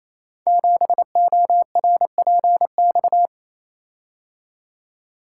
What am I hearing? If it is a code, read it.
Morse code, heard as a keyed tone, about 29 wpm, 700 Hz. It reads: 7ORPX
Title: Code Groups - 5 Letter:Numbers - 28wpm